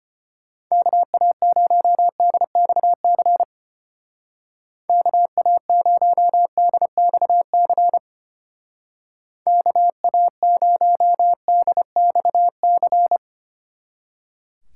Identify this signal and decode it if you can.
Morse code, heard as a keyed tone, about 32 wpm, 700 Hz. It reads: KA0DXC KA0DXC KA0DXC